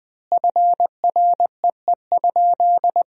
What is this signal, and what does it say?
Morse code, heard as a keyed tone, about 20 wpm, 700 Hz. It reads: FREE?
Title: 3x Word-Spacing - Sentences from Top 500 Words - 20wpm